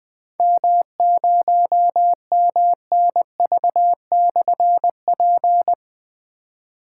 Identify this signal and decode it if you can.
Morse code, heard as a keyed tone, about 20 wpm, 700 Hz. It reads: M0MNV/P
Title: Difficult Call Signs 20wpm